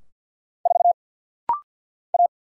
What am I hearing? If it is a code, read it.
Morse code, heard as a keyed tone, about 50 wpm, 690 Hz. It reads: 4 A
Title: Single Letter-Number Alt - Rapid-Fire 50wpm